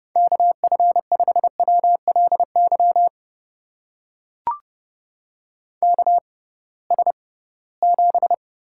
Morse code, heard as a keyed tone, about 31 wpm, spaced as for 20 wpm, 700 Hz. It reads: KF5WLY KS7
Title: USA Call Signs - 6x Character Spacing - 30wpm